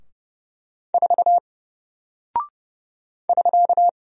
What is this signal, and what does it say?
Morse code, heard as a keyed tone, about 30 wpm, 700 Hz. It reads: 4 <SK>
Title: Single Character (Letter:Number:Punctuation:Prosign) 30wpm